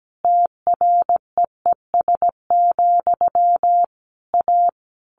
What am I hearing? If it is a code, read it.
Morse code, heard as a keyed tone, about 17 wpm, 700 Hz. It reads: TREES, A
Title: A Sound Of Thunder 17wpm